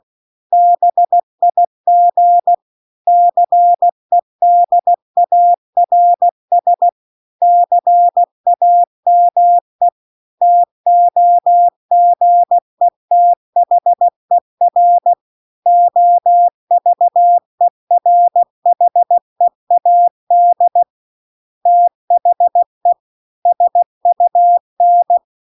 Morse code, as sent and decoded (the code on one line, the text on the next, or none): -... .. --. / -.-. -.. .- .-. ... / -.-. .- -- . / - --- --. - .... .-. / --- ...- . .-. .... .- -.. / - .... / ... ..- -.
BIG CEDARS CAME TOGETHER OVERHEAD THE SUN